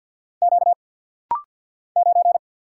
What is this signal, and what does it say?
Morse code, heard as a keyed tone, about 48 wpm, 700 Hz. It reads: Q 9
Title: Single Letter-Number - Rapid-Fire 50wpm